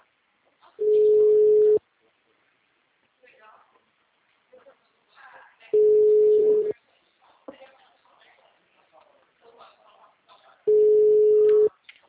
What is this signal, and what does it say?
Ringback tone (Europe, 425 Hz), with speech.
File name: out- -2012- - - .7258